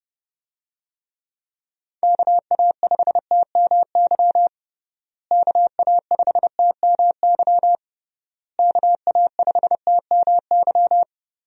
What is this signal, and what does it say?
Morse code, heard as a keyed tone, about 32 wpm, 700 Hz. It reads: KA5TMY KA5TMY KA5TMY
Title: USA Call Signs - 6 Characters - Repeated 3x - 30wpm